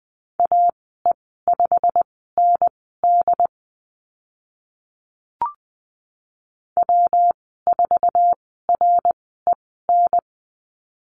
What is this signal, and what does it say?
Morse code, heard as a keyed tone, about 20 wpm, spaced as for 14 wpm, 700 Hz. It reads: AE5ND W4REN
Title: 193 Lesson 31g - Letter v - Call Signs 20wpm